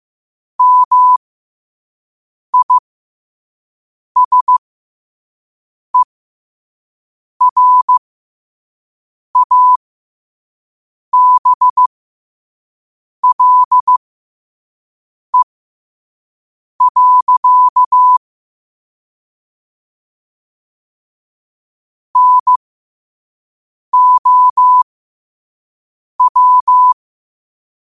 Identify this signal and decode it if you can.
Morse code, heard as a keyed tone, about 15 wpm, spaced as for 5 wpm, 1000 Hz. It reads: MISERABLE. NOW
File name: qotdfiles / 5wpm